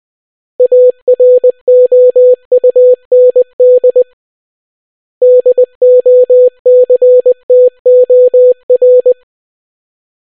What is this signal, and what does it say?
Morse code, heard as a keyed tone, about 20 wpm, 500 Hz. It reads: AROUND DOCTOR